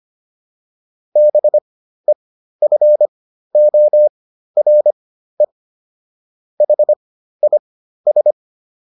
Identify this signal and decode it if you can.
Morse code, heard as a keyed tone, about 25 wpm, spaced as for 13 wpm, 600 Hz. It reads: BEFORE HIS